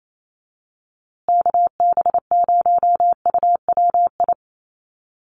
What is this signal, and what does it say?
Morse code, heard as a keyed tone, about 28 wpm, 700 Hz. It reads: KB0UWI
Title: USA Call Signs Repeated 2x 28wpm